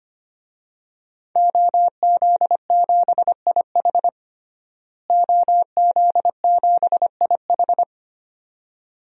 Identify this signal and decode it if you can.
Morse code, heard as a keyed tone, about 25 wpm, 700 Hz. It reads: OZ7IH OZ7IH
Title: DXCC-100 Call Signs - 5 Characters - Repeated 2x - 25wpm